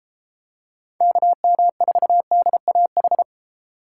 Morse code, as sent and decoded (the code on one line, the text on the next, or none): -.- -- ....- -.. .- ....
KM4DAH